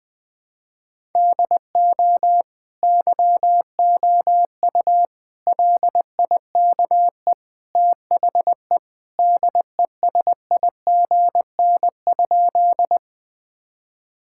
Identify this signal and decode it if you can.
Morse code, heard as a keyed tone, about 21 wpm, 700 Hz. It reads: DO YOU LIKE THE DESIGN?
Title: Sentences from Top 1000 Words - Review Part 5 - 20wpm